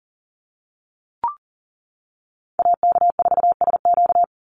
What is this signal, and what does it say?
Morse code, heard as a keyed tone, about 40 wpm, 700 Hz. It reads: AK4SQ
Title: Call Signs - 40wpm